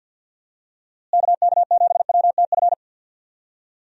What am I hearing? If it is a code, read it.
Morse code, heard as a keyed tone, about 52 wpm, 700 Hz. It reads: KK7WTF
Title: USA Call Signs - 6 Characters - Repeated 2x - 50wpm